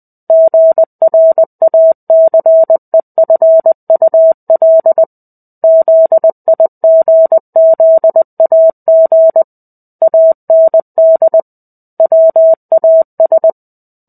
Morse code, heard as a keyed tone, about 20 wpm, 650 Hz. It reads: GRACEFUL ZIGZAG AND WAS
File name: 05-Chapter-05 20 WPM